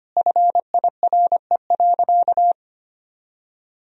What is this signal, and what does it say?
Morse code, heard as a keyed tone, about 25 wpm, 700 Hz. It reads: FIRE.
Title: Sentences from Top 1000 Words - Review Part 1 - 25wpm